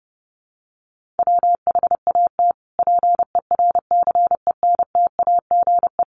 Morse code, heard as a keyed tone, about 30 wpm, 700 Hz. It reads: WHAT PERCENTAGE
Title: Sentences from Top 1300 Words 30wpm